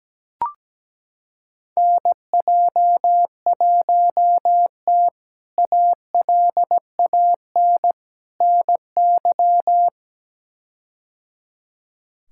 Morse code, heard as a keyed tone, about 17 wpm, 700 Hz. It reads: NJ1T ALAN NY
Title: K1USN SST QSO Practice - 17wpm